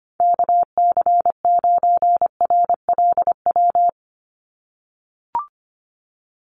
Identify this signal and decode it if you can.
Morse code, heard as a keyed tone, about 25 wpm, 700 Hz. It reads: KC9RLW
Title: USA Call Signs 25wpm